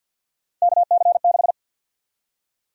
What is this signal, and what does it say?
Morse code, heard as a keyed tone, about 51 wpm, 700 Hz. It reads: KC6
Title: USA Call Sign Prefixes - 50wpm